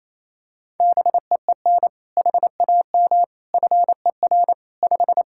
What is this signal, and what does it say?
Morse code, heard as a keyed tone, about 28 wpm, 700 Hz. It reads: BEEN HAM FER 5
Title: Practice QSO - Exchange 3 - More Info - 28wpm